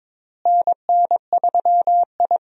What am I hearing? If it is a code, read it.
Morse code, heard as a keyed tone, about 22 wpm, 700 Hz. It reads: NN3I
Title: USA Call Signs - 4 Characters - Repeated 3x - 22wpm